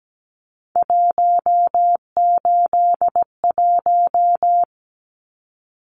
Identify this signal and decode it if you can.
Morse code, heard as a keyed tone, about 17 wpm, 700 Hz. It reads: 181
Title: Numbers - 3-Digit 17wpm